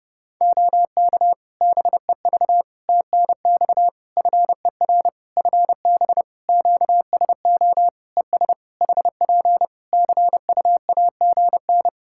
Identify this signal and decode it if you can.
Morse code, heard as a keyed tone, about 30 wpm, 700 Hz. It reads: OK BEV TNX FER FB QSO ES HP CUAGN
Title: Practice QSO - Exchange 4 - Ending - 30wpm